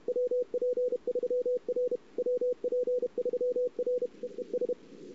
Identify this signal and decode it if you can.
Morse code, heard as a keyed tone, about 32 wpm, 500 Hz. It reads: WP3R WP3R EES
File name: CW / 2008-ss-cw-k5tr-op-n5rz -02:00